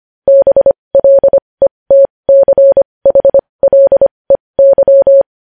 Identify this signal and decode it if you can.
Morse code, heard as a keyed tone, about 25 wpm, spaced as for 20 wpm, 550 Hz. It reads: BLETCHLEY
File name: podcast / News headlines-1x-25x20